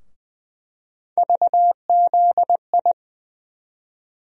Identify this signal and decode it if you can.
Morse code, heard as a keyed tone, about 20 wpm, 700 Hz. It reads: VZI
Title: USA Call Sign Postfixes - 3 Letters 20wpm